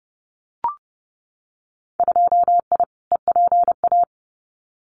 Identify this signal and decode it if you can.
Morse code, heard as a keyed tone, about 30 wpm, 700 Hz. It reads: 2I EPA
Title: Winter Field Day Exchanges 30wpm